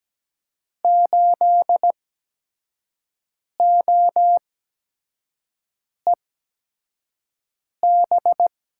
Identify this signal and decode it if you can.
Morse code, heard as a keyed tone, about 17 wpm, spaced as for 5 wpm, 700 Hz. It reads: 8OEB
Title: USA Call Signs - 8x Character Spacing - 17wpm